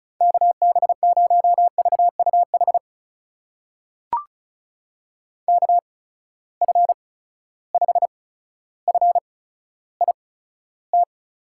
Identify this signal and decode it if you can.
Morse code, heard as a keyed tone, about 36 wpm, spaced as for 15 wpm, 700 Hz. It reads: KB0VUH KF5FIT
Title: USA Call Signs - 8x Character Spacing - 35wpm